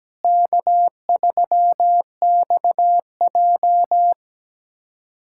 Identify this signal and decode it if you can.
Morse code, heard as a keyed tone, about 17 wpm, 700 Hz. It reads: K3XJ
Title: USA Callsign DE Callsign - 4 Characters - 17wpm